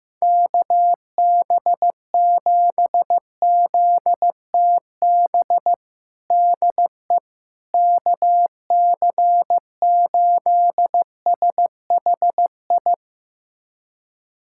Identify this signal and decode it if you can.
Morse code, heard as a keyed tone, about 15 wpm, 700 Hz. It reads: KB7ZTB DE KC8SHI
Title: USA Callsign DE Callsign - 6 Characters - 15wpm